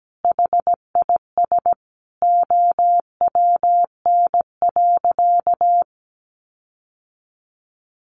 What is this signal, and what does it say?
Morse code, heard as a keyed tone, about 17 wpm, 700 Hz. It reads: HIS OWN.
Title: Sentences from Top 1400 Words 17wpm